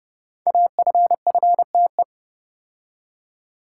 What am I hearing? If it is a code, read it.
Morse code, heard as a keyed tone, about 30 wpm, 700 Hz. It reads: AFFTE
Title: Code Groups - 5 Letters - 30wpm